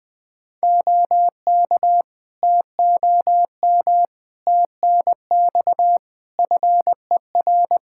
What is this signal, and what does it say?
Morse code, heard as a keyed tone, about 20 wpm, 700 Hz. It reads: OK TOM TNX FER
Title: Practice QSO - Exchange 4 - Ending - 20wpm